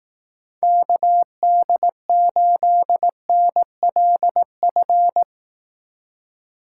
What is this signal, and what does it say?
Morse code, heard as a keyed tone, about 18 wpm, 700 Hz. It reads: KD8NLF
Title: Speed Racing - USA 2x3 Call Signs 15wpm